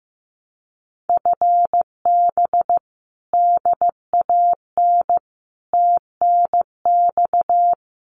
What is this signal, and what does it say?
Morse code, heard as a keyed tone, about 15 wpm, 700 Hz. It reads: FB DAN TNX
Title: Common QSO Phrases Encore 15wpm